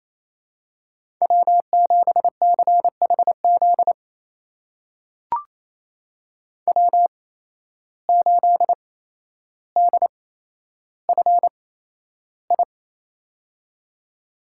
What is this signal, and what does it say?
Morse code, heard as a keyed tone, about 28 wpm, spaced as for 12 wpm, 700 Hz. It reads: W7CHZ W8DFI
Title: USA Call Signs - 5 Characters - 8x Character Spacing - 28wpm